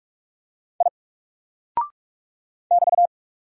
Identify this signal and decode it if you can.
Morse code, heard as a keyed tone, about 46 wpm, 700 Hz. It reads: I <BT>
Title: Single Character (Letter:Number:Punctuation:Prosign) 45wpm